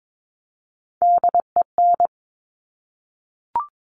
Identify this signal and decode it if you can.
Morse code, heard as a keyed tone, about 22 wpm, 700 Hz. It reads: DEN
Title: Top Three Letter Combinations in Words 22wpm